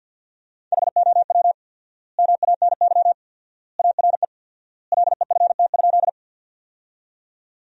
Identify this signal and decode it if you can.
Morse code, heard as a keyed tone, about 52 wpm, 700 Hz. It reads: HOW MANY ARE LEFT?
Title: 4x Word-Spacing - Sentences from Top 300 Words - 50wpm